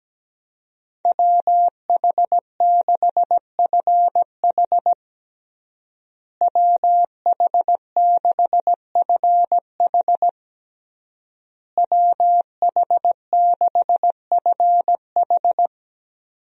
Morse code, heard as a keyed tone, about 17 wpm, 700 Hz. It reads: WH6FH WH6FH WH6FH
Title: USA Call Signs - 5 Characters - Repeated 3x - 17wpm